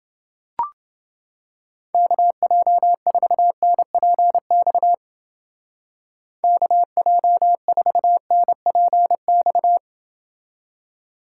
Morse code, heard as a keyed tone, about 29 wpm, 700 Hz. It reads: KJ4NPX KJ4NPX